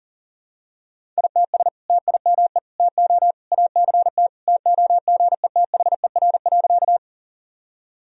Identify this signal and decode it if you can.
Morse code, heard as a keyed tone, about 40 wpm, 700 Hz. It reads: ITS TIME TO ACT TOGETHER.
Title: Sentences from Top 1000 Words - Review Part 1 - 40wpm